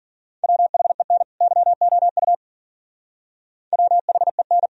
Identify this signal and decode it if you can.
Morse code, heard as a keyed tone, about 51 wpm, 700 Hz. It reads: WHEN YOU WHEN